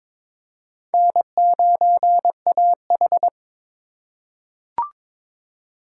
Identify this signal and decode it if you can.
Morse code, heard as a keyed tone, about 22 wpm, 700 Hz. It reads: N9AH
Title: USA Call Signs - 3x Character Spacing - 22wpm